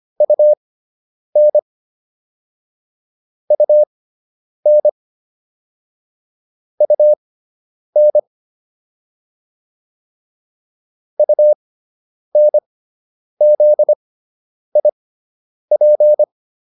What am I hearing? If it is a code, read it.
Morse code, heard as a keyed tone, about 25 wpm, spaced as for 9 wpm, 600 Hz. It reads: UN UN UN UNZIP